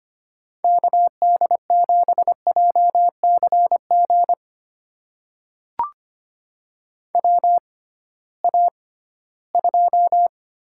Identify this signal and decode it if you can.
Morse code, heard as a keyed tone, about 25 wpm, spaced as for 16 wpm, 700 Hz. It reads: KD7JCG WA2
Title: USA Call Signs - 6 Characters - 6x Character Spacing - 25wpm